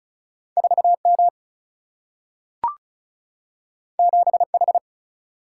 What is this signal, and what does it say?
Morse code, heard as a keyed tone, about 36 wpm, 700 Hz. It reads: 4M 7H
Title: Single-Letter Single-Number 35wpm